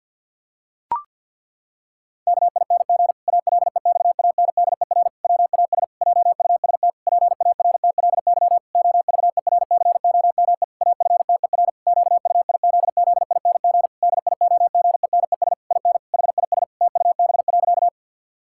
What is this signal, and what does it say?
Morse code, heard as a keyed tone, about 46 wpm, 700 Hz. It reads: KING ALEXANDER WAS JUST PARTLY OVERCOME AFTER QUIZZING DIOGENES IN HIS TUB.
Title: Pangram Challenge v2 - 50wpm